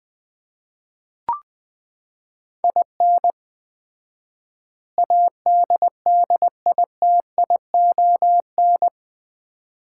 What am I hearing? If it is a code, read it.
Morse code, heard as a keyed tone, about 21 wpm, 700 Hz. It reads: IN ADDITION
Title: 4x Word-Spacing - Sets of 4 Words - 20wpm